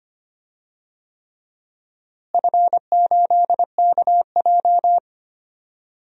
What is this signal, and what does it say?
Morse code, heard as a keyed tone, about 25 wpm, 700 Hz. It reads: F8KJ
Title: DXCC-100 Call Signs - 4 Characters - Repeated 3x - 25wpm